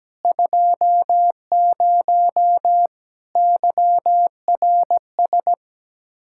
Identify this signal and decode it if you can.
Morse code, heard as a keyed tone, about 17 wpm, 700 Hz. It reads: 20 YRS
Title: Practice QSO - Exchange 3 - More Info - 17wpm